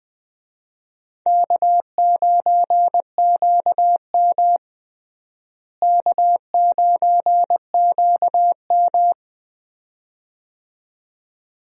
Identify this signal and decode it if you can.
Morse code, heard as a keyed tone, about 20 wpm, 700 Hz. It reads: K9QM K9QM
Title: USA Call Signs - 4 Characters - Repeated 2x - 20wpm